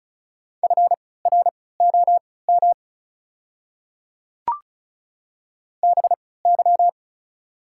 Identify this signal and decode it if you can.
Morse code, heard as a keyed tone, about 36 wpm, spaced as for 20 wpm, 690 Hz. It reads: FROM BY
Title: 3x Character Spacing - Top 100 Words - 35wpm